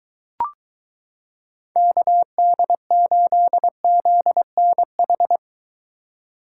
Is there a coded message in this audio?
Yes: Morse code from 1.8 to 5.4 s, heard as a keyed tone, about 23 wpm, 700 Hz. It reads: KD8ZNH